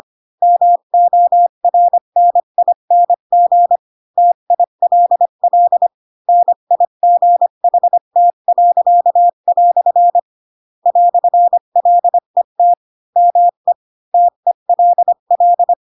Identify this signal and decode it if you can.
Morse code, heard as a keyed tone, about 25 wpm, 700 Hz. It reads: MORNING TILL NIGHT." "LET ME TELL
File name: Pinocchio / Pinocchio ch4 20 25 ik1hge